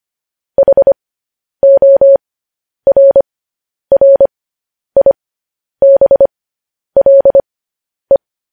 Morse code, heard as a keyed tone, about 25 wpm, spaced as for 10 wpm, 550 Hz. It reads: HORRIBLE
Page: News Headlines in Morse Code at 10 WPM